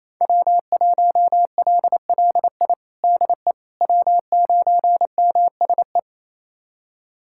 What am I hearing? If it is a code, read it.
Morse code, heard as a keyed tone, about 27 wpm, 700 Hz. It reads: W1LLI DE W9MSE
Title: USA Callsign DE Callsign - 5 Characters - 28wpm